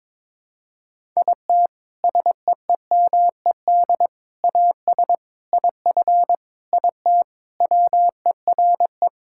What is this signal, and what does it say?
Morse code, heard as a keyed tone, about 22 wpm, 700 Hz. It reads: IT SEEMED AS IF IT WERE
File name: practice / 22wpm / Sentences with 4-Word Phrases - 22wpm